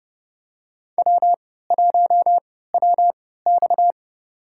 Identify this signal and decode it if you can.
Morse code, heard as a keyed tone, about 31 wpm, spaced as for 17 wpm, 700 Hz. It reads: W1WX
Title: USA Call Signs - 4 Characters - 3x Character Spacing - 30wpm